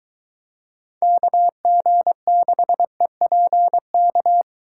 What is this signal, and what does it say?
Morse code, heard as a keyed tone, about 22 wpm, 700 Hz. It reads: KG6EPK